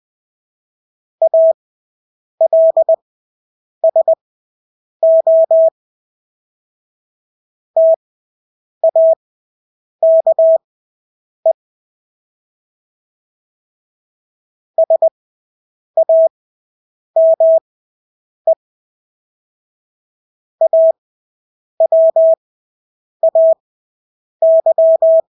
Morse code, heard as a keyed tone, about 20 wpm, spaced as for 8 wpm, 650 Hz. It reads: ALSO TAKE SAME AWAY
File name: Engelske ord 20/8 wpm